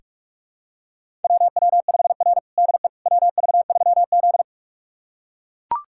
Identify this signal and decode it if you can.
Morse code, heard as a keyed tone, about 44 wpm, 690 Hz. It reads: WW5R DE WV3Z
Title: USA Callsign DE Callsign - 4 Characters - 45wpm